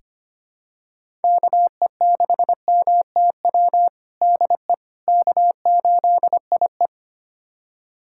Morse code, heard as a keyed tone, about 25 wpm, 700 Hz. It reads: KE6MTW DE K8IE
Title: USA Callsign DE Callsign 25wpm